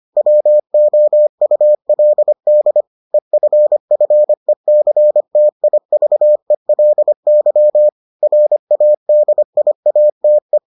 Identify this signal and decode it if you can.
Morse code, heard as a keyed tone, about 25 wpm, 600 Hz. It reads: WOULD EFFECTIVELY RADIATE